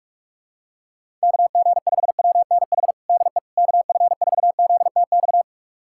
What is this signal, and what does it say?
Morse code, heard as a keyed tone, about 46 wpm, 700 Hz. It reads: KG5WNH DE KF4ZTX
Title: USA Callsign DE Callsign 45wpm